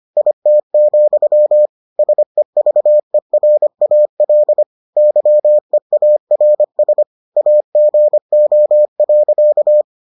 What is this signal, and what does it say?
Morse code, heard as a keyed tone, about 25 wpm, 600 Hz. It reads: IT, SEVERAL YEARS AGO.